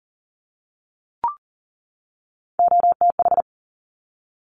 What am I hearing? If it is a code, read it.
Morse code, heard as a keyed tone, about 42 wpm, 700 Hz. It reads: OTH
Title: Top Three Letter Combinations in Words 40wpm